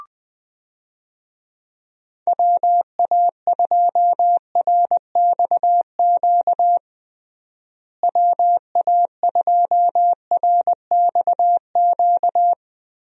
Morse code, heard as a keyed tone, about 21 wpm, 700 Hz. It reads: WA2RXQ WA2RXQ
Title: USA Call Signs - 6 Characters - Repeated 2x - 20wpm